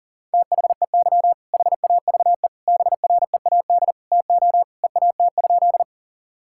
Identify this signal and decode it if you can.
Morse code, heard as a keyed tone, about 37 wpm, 700 Hz. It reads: THEY HAVE BREAD TO EAT?
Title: Sentences from Top 1600 Words 40wpm